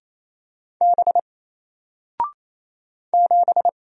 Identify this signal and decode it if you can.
Morse code, heard as a keyed tone, about 28 wpm, 700 Hz. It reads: B 7
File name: Single-Letter-Number-28wpm